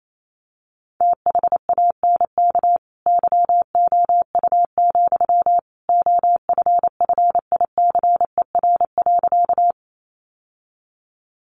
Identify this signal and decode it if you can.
Morse code, heard as a keyed tone, about 28 wpm, 700 Hz. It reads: THANK YOU, OFFICER.
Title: Sentences from Top 1200 Words 28wpm